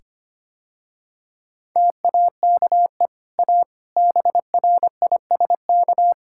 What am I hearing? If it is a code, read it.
Morse code, heard as a keyed tone, about 25 wpm, 700 Hz. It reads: TAKE A BRISK
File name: Easy Sentences with Hints - 25wpm